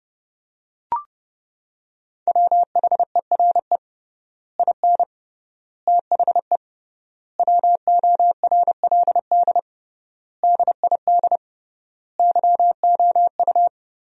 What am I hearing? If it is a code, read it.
Morse code, heard as a keyed tone, about 31 wpm, 700 Hz. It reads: WHERE IN THE WORLD DID YOU
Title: 3x Word-Spacing - Sentences from Top 200 Words - 30wpm